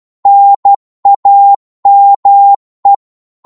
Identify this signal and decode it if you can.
Morse code, heard as a keyed tone, about 12 wpm, 800 Hz. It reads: NAME
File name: Morse / qso-12wpm